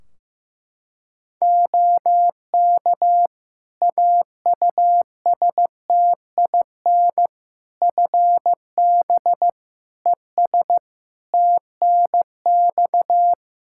Morse code, heard as a keyed tone, about 15 wpm, 700 Hz. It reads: OK AUSTIN FB ES TNX
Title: Practice QSO - Exchange 2 - Info - 15wpm